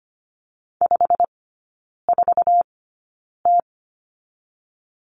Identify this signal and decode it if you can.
Morse code, heard as a keyed tone, about 25 wpm, spaced as for 9 wpm, 690 Hz. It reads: 54T
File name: Lesson 15e - Number 5 - Three Characters 25wpm